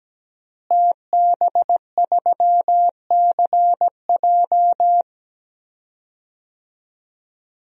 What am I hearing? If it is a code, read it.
Morse code, heard as a keyed tone, about 17 wpm, 700 Hz. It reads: TB3CJ
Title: DXCC-100 Call Signs - 5 Characters - Repeated 2x - 17wpm